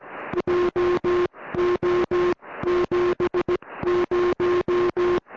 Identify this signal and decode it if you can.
Morse code, heard as a keyed tone, about 17 wpm, spaced as for 15 wpm, 360 Hz, with noise. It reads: JO 70
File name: au / oz9pz ok1rk 70cm au